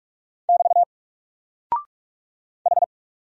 Morse code, heard as a keyed tone, about 44 wpm, 700 Hz. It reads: <BT> H
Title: Single Character (Letter:Number:Punctuation:Prosign) Alt 45wpm